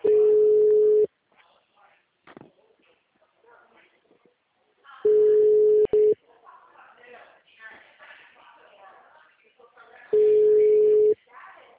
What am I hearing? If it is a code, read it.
Ringback tone (Europe, 425 Hz), with speech.
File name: out- -2005- - - .56570